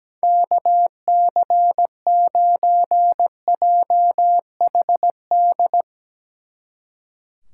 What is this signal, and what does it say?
Morse code, heard as a keyed tone, about 17 wpm, 700 Hz. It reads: KC9JHD
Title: Speed Racing - USA 2x3 Call Signs 17wpm